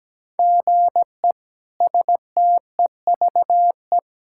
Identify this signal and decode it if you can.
Morse code, heard as a keyed tone, about 17 wpm, 700 Hz. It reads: GE STEVE
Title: K1USN SST QSO Practice - 17wpm